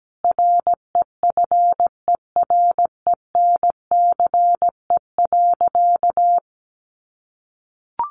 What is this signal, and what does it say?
Morse code, heard as a keyed tone, about 17 wpm, 700 Hz. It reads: REFERENCE.
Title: Sentences from Top 1200 Words 17wpm